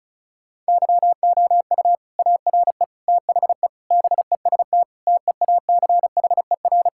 Morse code, heard as a keyed tone, about 36 wpm, 700 Hz. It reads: YOU ARE THE BEST TEACHER
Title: Sentences from Top 1000 Words - Review Part 6 - 35wpm